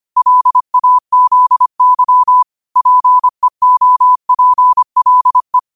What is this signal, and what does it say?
Morse code, heard as a keyed tone, about 25 wpm, 1000 Hz. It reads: LAZY PEOPLE